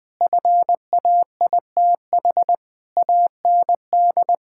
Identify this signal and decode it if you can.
Morse code, heard as a keyed tone, about 20 wpm, 700 Hz. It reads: FAITH AND
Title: Sentences with 9-Word Phrases - Spoken Beginning - 20wpm